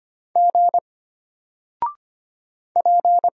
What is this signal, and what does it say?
Morse code, heard as a keyed tone, about 25 wpm, 700 Hz. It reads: G P
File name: Single-Letter-Number-25wpm